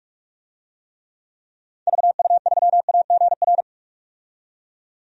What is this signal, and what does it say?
Morse code, heard as a keyed tone, about 43 wpm, 700 Hz. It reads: VU3AGR
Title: DXCC-100 Call Signs - 6 Characters - 45wpm